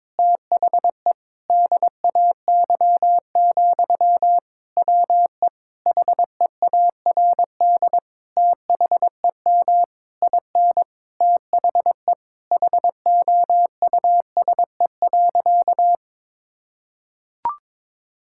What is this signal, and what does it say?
Morse code, heard as a keyed tone, about 22 wpm, 700 Hz. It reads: THE DAY, WE HEARD THEM IN THE HOUSE.
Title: Sentences from Top 1000 Words - Review Part 5 - 22wpm